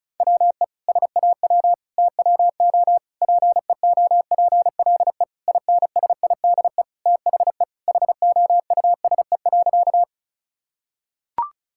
Morse code, heard as a keyed tone, about 34 wpm, 700 Hz. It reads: WE SAW TWO PEOPLE INSIDE THE HOUSE.
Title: Sentences from Top 1000 Words - Review Part 6 - 35wpm